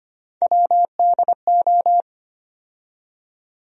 Morse code, heard as a keyed tone, about 25 wpm, 700 Hz. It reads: WDO
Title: USA Call Sign Postfixes - 3 Letters 25wpm